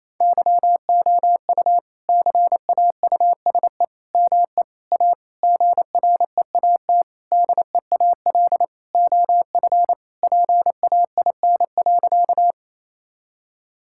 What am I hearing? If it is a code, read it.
Morse code, heard as a keyed tone, about 28 wpm, 700 Hz. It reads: YOU CAUSE ME A GREAT DEAL OF PAIN.
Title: Sentences from Top 1600 Words 28wpm